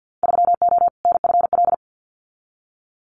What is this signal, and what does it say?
Morse code, heard as a keyed tone, about 50 wpm, 700 Hz. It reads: 3O NFL
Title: Winter Field Day Exchanges 50wpm